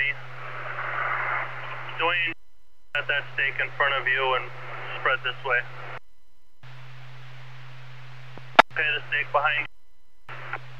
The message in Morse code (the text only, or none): none